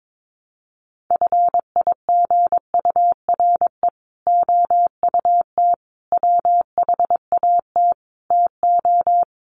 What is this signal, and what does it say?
Morse code, heard as a keyed tone, about 22 wpm, 700 Hz. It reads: FIGURE OUT WHAT TO
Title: Sets of 5 Words Encore2 22wpm